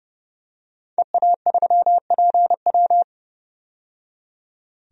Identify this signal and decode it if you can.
Morse code, heard as a keyed tone, about 30 wpm, 700 Hz. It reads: EA3PW
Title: DXCC-100 Call Signs - 5 Characters - 30wpm